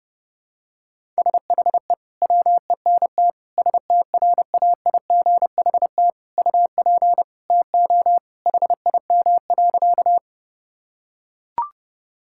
Morse code, heard as a keyed tone, about 30 wpm, 700 Hz. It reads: SHE WENT STRAIGHT UP TO HIM.
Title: Sentences from Top 1000 Words - Review Part 6 - 30wpm